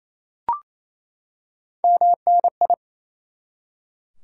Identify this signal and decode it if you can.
Morse code, heard as a keyed tone, about 28 wpm, 700 Hz. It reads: MNI
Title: Top 90 QSO Elements 28wpm